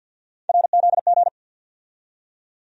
Morse code, heard as a keyed tone, about 52 wpm, 700 Hz. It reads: RZG
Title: USA Call Sign Postfixes - 3 Letters 50wpm